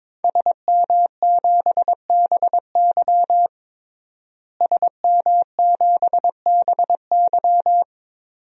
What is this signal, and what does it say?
Morse code, heard as a keyed tone, about 22 wpm, 700 Hz. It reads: SM7BY SM7BY
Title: DXCC-100 Call Signs - 5 Characters - Repeated 2x - 22wpm